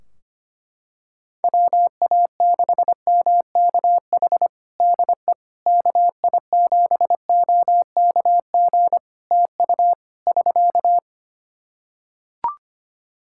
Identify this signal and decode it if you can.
Morse code, heard as a keyed tone, about 25 wpm, 700 Hz. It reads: WA6MKH DE KI7OKG TU <SK>
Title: Practice QSO - Exchange 4 - Ending - 25wpm